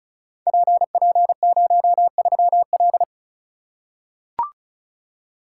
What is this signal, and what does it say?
Morse code, heard as a keyed tone, about 36 wpm, 700 Hz. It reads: PP03L
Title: Code Groups - 5 Letter:Numbers - 35wpm